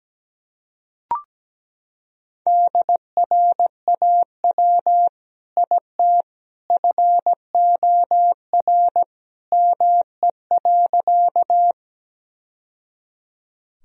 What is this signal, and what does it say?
Morse code, heard as a keyed tone, about 17 wpm, 700 Hz. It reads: DRAW IT FOR ME.
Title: Sentences from Top 1000 Words - Review Part 1 - 17wpm